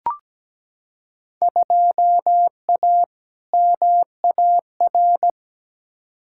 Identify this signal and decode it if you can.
Morse code, heard as a keyed tone, about 17 wpm, 700 Hz. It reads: 2A MAR
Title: Field Day Exchanges - 17wpm